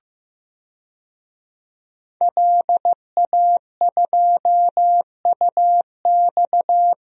Morse code, heard as a keyed tone, about 15 wpm, 700 Hz. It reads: LA2UX
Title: DXCC-100 Call Signs - 5 Characters - 15wpm